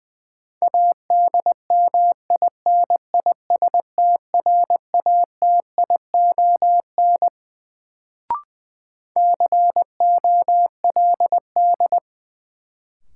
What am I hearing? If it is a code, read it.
Morse code, heard as a keyed tone, about 20 wpm, 700 Hz. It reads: ADMINISTRATION COLD
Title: Sets of 5 Words - New Word Drill - 20wpm